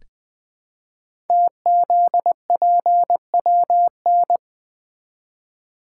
Morse code, heard as a keyed tone, about 20 wpm, 700 Hz. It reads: TZPWN
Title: Code Groups - 5 Letter:Numbers - 20wpm